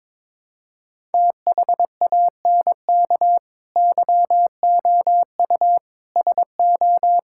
Morse code, heard as a keyed tone, about 22 wpm, 700 Hz. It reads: THANK YOU SO
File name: practice / 22wpm / Build A Phrase - 5-Word Phrases - 22wpm